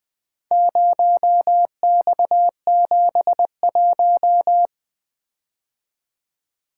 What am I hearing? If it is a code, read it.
Morse code, heard as a keyed tone, about 20 wpm, 700 Hz. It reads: 0X71
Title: Four Letter-Number Combinations - 20wpm